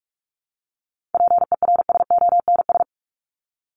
Morse code, heard as a keyed tone, about 44 wpm, 700 Hz. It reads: PERSONS
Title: ES:S Suffix - Top 100 Words with Base Words 45wpm